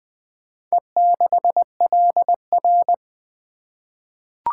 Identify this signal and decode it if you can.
Morse code, heard as a keyed tone, about 20 wpm, 700 Hz. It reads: E6LR
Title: Four Letter-Number Combinations - 20wpm